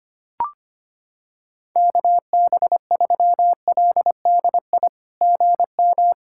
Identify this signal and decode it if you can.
Morse code, heard as a keyed tone, about 25 wpm, 700 Hz. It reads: KB3LDI GM
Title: SOTA QSO Practice 25wpm